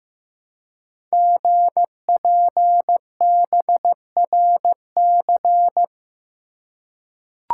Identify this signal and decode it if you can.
Morse code, heard as a keyed tone, about 15 wpm, 700 Hz. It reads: GPBRC
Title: Code Groups - 5 Letters - 15wpm